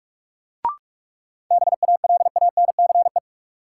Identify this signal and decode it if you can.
Morse code, heard as a keyed tone, about 46 wpm, 700 Hz. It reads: BALANCE
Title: Sentences from Top 900 Words - New Word Drill - 45wpm